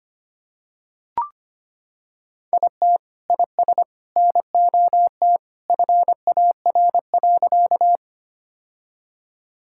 Morse code, heard as a keyed tone, about 25 wpm, 700 Hz. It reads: IT IS NOT FAR.
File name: practice / 25wpm / Sentences with 3-Word Phrases - 25wpm